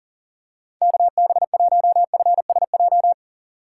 Morse code, heard as a keyed tone, about 40 wpm, 700 Hz. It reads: KB1FSJ
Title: USA Call Signs - 6 Characters - 2x Character Spacing - 40wpm